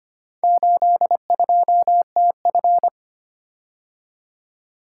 Morse code, heard as a keyed tone, about 25 wpm, 700 Hz. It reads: 82TF
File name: practice / 25wpm / Four Letter-Number Combinations - 25wpm